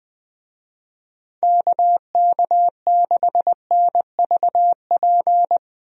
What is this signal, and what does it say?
Morse code, heard as a keyed tone, about 20 wpm, 700 Hz. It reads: KK6NVP
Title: USA Call Signs - 6 Characters - Repeated 3x - 20wpm